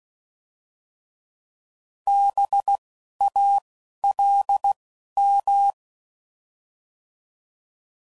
Morse code, heard as a keyed tone, about 16 wpm, 800 Hz. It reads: B A L M